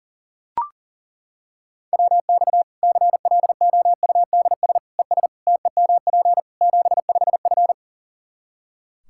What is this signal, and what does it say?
Morse code, heard as a keyed tone, about 40 wpm, 700 Hz. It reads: WX CLOUDS ES TEMP 75F
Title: K1USN SST QSO Practice 40wpm